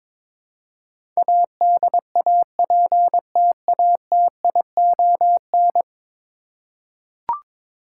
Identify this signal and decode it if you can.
Morse code, heard as a keyed tone, about 22 wpm, 700 Hz. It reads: ADAPTATION
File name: practice / 22wpm / 10 Letter Words - First 3 Letters Given - 22wpm